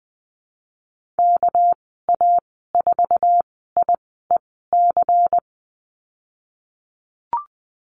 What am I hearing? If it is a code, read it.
Morse code, heard as a keyed tone, about 20 wpm, spaced as for 15 wpm, 700 Hz. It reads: KA4IEC
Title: 213 Lesson 34g - Number 8 - Call Signs 20wpm